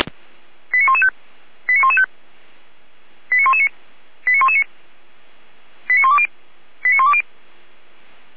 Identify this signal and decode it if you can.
Five-tone selective call (ZVEI), heard as quick runs of five tones, tones of about 70 ms. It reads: 89115 89115 89119 89119 89120 89120
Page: SDR/ZVEI